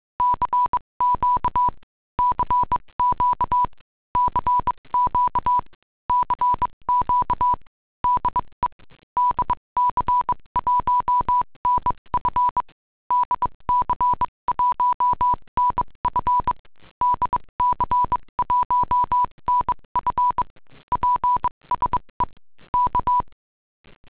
Morse code, heard as a keyed tone, about 22 wpm, spaced as for 19 wpm, 990 Hz. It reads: CQ CQ CQ CQ DE DC1NF DC1NF DC1NF PSE K